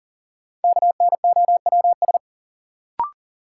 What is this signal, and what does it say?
Morse code, heard as a keyed tone, about 40 wpm, 700 Hz. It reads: KNOWS
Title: Sentences from Top 1000 Words - Review Part 1 - New Word Drill - 40wpm